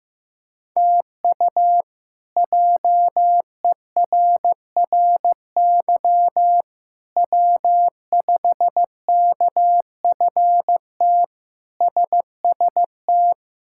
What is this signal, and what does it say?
Morse code, heard as a keyed tone, about 15 wpm, 700 Hz. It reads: TU JERRY W5KFT SST
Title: K1USN SST QSO Practice - 15wpm